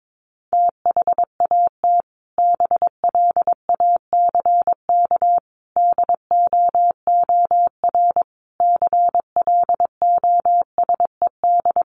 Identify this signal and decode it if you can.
Morse code, heard as a keyed tone, about 22 wpm, 700 Hz. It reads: THAT BLACK DOOR CLOSED
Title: Sentences from Top 1400 Words 22wpm